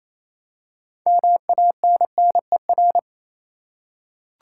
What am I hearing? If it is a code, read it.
Morse code, heard as a keyed tone, about 28 wpm, 700 Hz. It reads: MANNER
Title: Six Letter Words - 28wpm